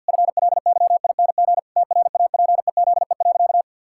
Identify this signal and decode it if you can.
Morse code, heard as a keyed tone, about 51 wpm, 700 Hz. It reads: FLYING TRAPEZE.
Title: Pangram Challenge v2 - 50wpm